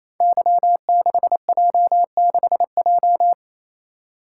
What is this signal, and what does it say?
Morse code, heard as a keyed tone, about 28 wpm, 700 Hz. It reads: Y6J6J
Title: Code Groups - 5 Letter:Numbers - 28wpm